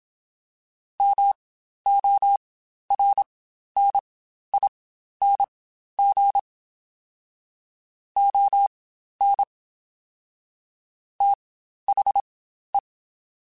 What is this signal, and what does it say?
Morse code, heard as a keyed tone, about 27 wpm, spaced as for 12 wpm, 800 Hz. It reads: MORNING ON THE
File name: Spaced Characters / Spaced Characters-mp3s / Weather4 24 sp @25